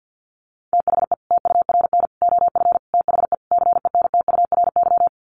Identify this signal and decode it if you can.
Morse code, heard as a keyed tone, about 48 wpm, 700 Hz. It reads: THE TURN OF THE CENTURY
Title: Sets of 5 Words 50wpm